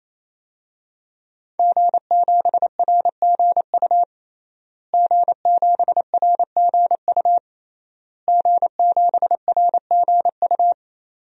Morse code, heard as a keyed tone, about 28 wpm, 700 Hz. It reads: G7RGU G7RGU G7RGU
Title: DXCC-100 Call Signs - 5 Characters - Repeated 3x - 28wpm